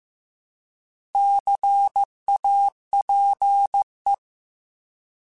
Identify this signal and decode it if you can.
Morse code, heard as a keyed tone, about 15 wpm, 800 Hz. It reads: CAPE
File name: Alphabetical / Alphabetical mp3s / Cs @13